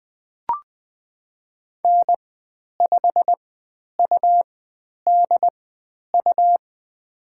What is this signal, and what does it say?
Morse code, heard as a keyed tone, about 20 wpm, spaced as for 10 wpm, 700 Hz. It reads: N5UDU
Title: 102 Lesson 17g - Letter u - Call Signs 20wpm